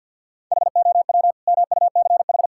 Morse code, heard as a keyed tone, about 46 wpm, 700 Hz. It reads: HOW MUCH
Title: Sentences from Top 1000 Words - Review Part 5 - 50wpm